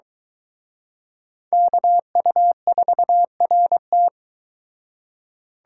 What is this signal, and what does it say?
Morse code, heard as a keyed tone, about 23 wpm, 700 Hz. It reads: KU4RT